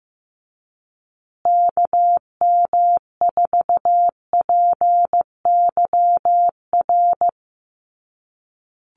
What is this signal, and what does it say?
Morse code, heard as a keyed tone, about 15 wpm, 700 Hz. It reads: KM4PYR
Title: USA Call Signs Repeated 2x 15wpm